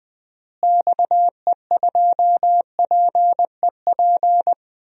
Morse code, heard as a keyed tone, about 20 wpm, 700 Hz. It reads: XE2PEP
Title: DXCC-100 Call Signs - 6 Characters - Repeated 2x - 20wpm